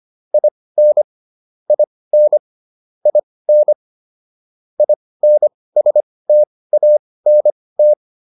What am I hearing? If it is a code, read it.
Morse code, heard as a keyed tone, about 25 wpm, spaced as for 18 wpm, 600 Hz. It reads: IN IN IN INSTANT